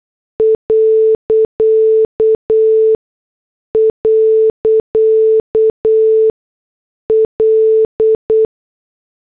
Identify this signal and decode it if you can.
Morse code, heard as a keyed tone, about 8 wpm, spaced as for 6 wpm, 440 Hz. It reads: ..L
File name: morse / headlines